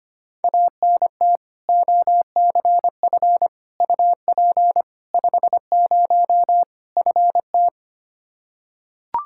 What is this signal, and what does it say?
Morse code, heard as a keyed tone, about 25 wpm, 700 Hz. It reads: ANT OCF UP 50 FT
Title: Practice QSO - Exchange 2 - Info - 25wpm